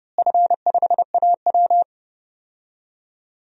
Morse code, heard as a keyed tone, about 30 wpm, 700 Hz. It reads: F5AW
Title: DXCC-100 Call Signs - 4 Characters - Repeated 2x - 30wpm